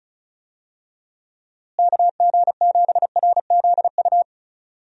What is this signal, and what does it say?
Morse code, heard as a keyed tone, about 34 wpm, 700 Hz. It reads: KG7RZU
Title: USA Call Signs - 6 Characters - Repeated 3x - 35wpm